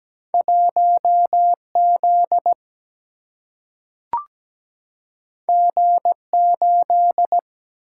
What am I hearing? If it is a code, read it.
Morse code, heard as a keyed tone, about 17 wpm, 700 Hz. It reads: 1Z G8
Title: Single-Letter Single-Number 17wpm